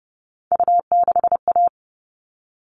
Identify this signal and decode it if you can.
Morse code, heard as a keyed tone, about 28 wpm, 700 Hz. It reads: U6A
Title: All Cut Numbers - 3-Digit 30wpm